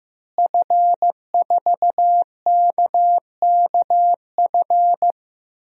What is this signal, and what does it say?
Morse code, heard as a keyed tone, about 15 wpm, 700 Hz. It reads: F4KKF
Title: DXCC-100 Call Signs - 5 Characters - Repeated 2x - 15wpm